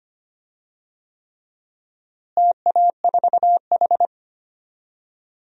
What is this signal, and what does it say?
Morse code, heard as a keyed tone, about 25 wpm, 700 Hz. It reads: TA4H
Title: DXCC-100 Call Signs - 4 Characters - Repeated 3x - 25wpm